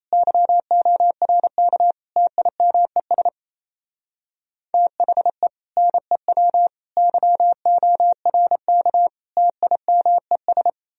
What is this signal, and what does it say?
Morse code, heard as a keyed tone, about 32 wpm, 700 Hz. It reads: YORK TIMES THE NEW YORK TIMES
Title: Speed-Racing - Sets of 4 Words 28wpm